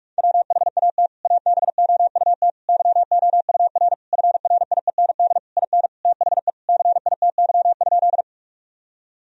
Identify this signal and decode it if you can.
Morse code, heard as a keyed tone, about 46 wpm, 700 Hz. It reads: WHAT ABOUT YOUR FRIEND IN THE CITY?
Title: Sentences from Top 1000 Words - Review Part 1 - 45wpm